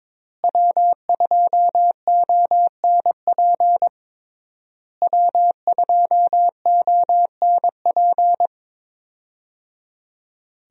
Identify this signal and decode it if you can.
Morse code, heard as a keyed tone, about 22 wpm, 700 Hz. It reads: W2ONP W2ONP
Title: USA Call Signs - 5 Characters - Repeated 3x - 22wpm